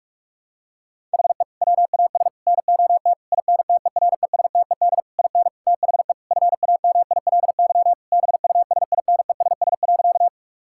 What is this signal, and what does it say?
Morse code, heard as a keyed tone, about 41 wpm, 700 Hz. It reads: HE WAS NOT INTERESTED IN THE FAMILY BUSINESS.
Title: Sentences from Top 1000 Words - Review Part 6 - 45wpm